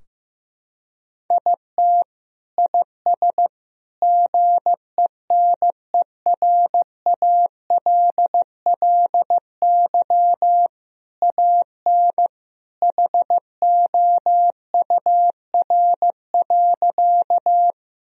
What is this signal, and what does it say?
Morse code, heard as a keyed tone, about 16 wpm, 700 Hz. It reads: IT IS GENERALLY AN HOUR.
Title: Sentences from Top 1000 Words - Review Part 4 - 15wpm